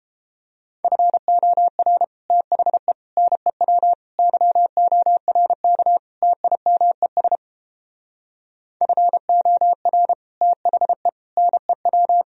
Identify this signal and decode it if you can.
Morse code, heard as a keyed tone, about 29 wpm, 700 Hz. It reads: FOR THE NEW YORK TIMES FOR THE NEW